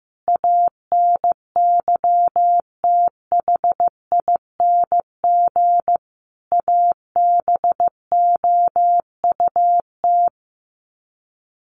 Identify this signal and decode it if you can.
Morse code, heard as a keyed tone, about 15 wpm, 700 Hz. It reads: ANYTHING ABOUT
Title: Sets of 5 Words Encore2 15wpm